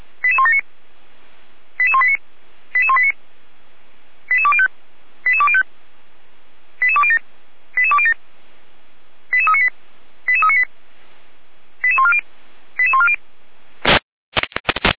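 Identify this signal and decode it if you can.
Five-tone selective call (ZVEI), heard as quick runs of five tones, tones of about 70 ms. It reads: 88179 88189 88189 88205 88205 88207 88207 88398 88398 88140 88140
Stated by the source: ZVEI selective call received by SDR, logged by the Feuerwehr Orlamünde (fire brigade) audio server.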